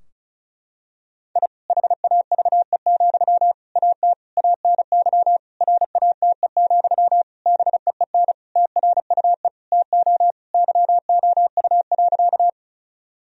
Morse code, heard as a keyed tone, about 36 wpm, 700 Hz. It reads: I HAVE, AT ANY RATE, BEEN TRUE TO YOU.
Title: Sentences from Top 1000 Words - Review Part 4 - 35wpm